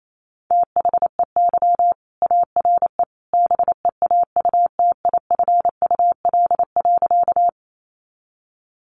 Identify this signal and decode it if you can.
Morse code, heard as a keyed tone, about 28 wpm, 700 Hz. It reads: THEY ARE BEAUTIFUL.
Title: Sentences from Top 900 Words 28wpm